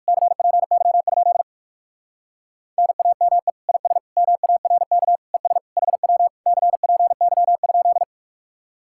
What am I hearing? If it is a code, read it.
Morse code, heard as a keyed tone, about 59 wpm, 700 Hz. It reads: CPY? NAME IS MARK ES HW CPY?